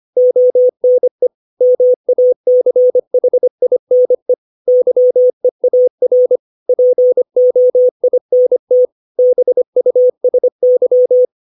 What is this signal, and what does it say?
Morse code, heard as a keyed tone, about 25 wpm, 500 Hz. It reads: ONE MACHINE YEAR POINT BUSY